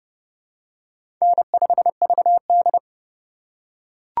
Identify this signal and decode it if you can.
Morse code, heard as a keyed tone, about 31 wpm, 700 Hz. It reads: N5VD
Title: USA Call Signs - 4 Characters - 3x Character Spacing - 30wpm